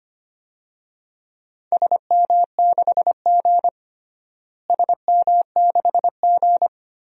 Morse code, heard as a keyed tone, about 25 wpm, 700 Hz. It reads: SM6G SM6G
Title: DXCC-100 Call Signs - 4 Characters - Repeated 3x - 25wpm